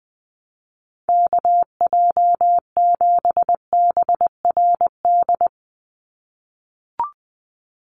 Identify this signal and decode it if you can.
Morse code, heard as a keyed tone, about 20 wpm, 700 Hz. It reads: KJ7BRD
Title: USA Call Signs 20wpm